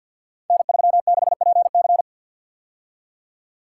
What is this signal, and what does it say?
Morse code, heard as a keyed tone, about 50 wpm, 700 Hz. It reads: N36PC
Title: Code Groups - 5 Letter:Numbers - 50wpm